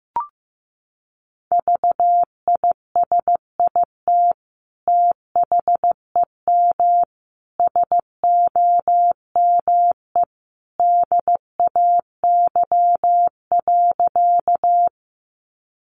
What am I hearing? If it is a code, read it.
Morse code, heard as a keyed tone, about 15 wpm, 700 Hz. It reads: VISIT THEM SOME DAY.
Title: Sentences from Top 900 Words 15wpm